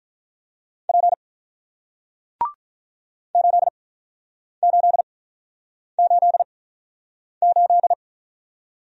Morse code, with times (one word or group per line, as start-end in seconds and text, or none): 0.9-1.1 P
3.4-7.9 8888